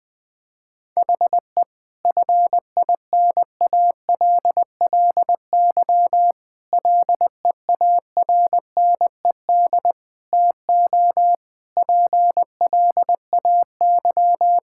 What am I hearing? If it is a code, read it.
Morse code, heard as a keyed tone, about 21 wpm, 700 Hz. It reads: HE FINALLY LEARNED TO PLAY
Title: Difficult Sentences with Hints - 20wpm